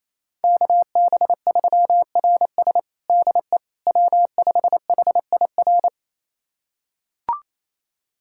Morse code, heard as a keyed tone, about 28 wpm, 700 Hz. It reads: KB3RS DE W5HIR
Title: USA Callsign DE Callsign 28wpm